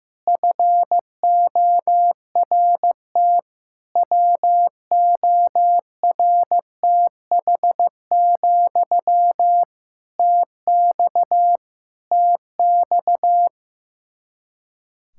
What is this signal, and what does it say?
Morse code, heard as a keyed tone, about 15 wpm, 700 Hz. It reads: FORT WORTH, TX TX
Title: Practice QSO - Exchange 1 - Beginning - 15wpm